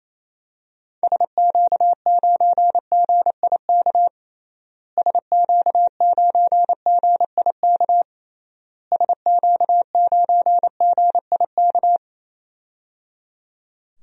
Morse code, heard as a keyed tone, about 28 wpm, 700 Hz. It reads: SQ9GIK SQ9GIK SQ9GIK
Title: DXCC-100 Call Signs - 6 Characters - Repeated 3x - 28wpm